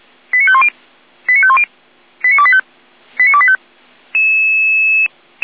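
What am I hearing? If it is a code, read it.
Five-tone selective call (ZVEI), heard as quick runs of five tones, tones of about 70 ms. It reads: 89510 89510 89285 89285
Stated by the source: ZVEI five-tone selective call received on an SDR.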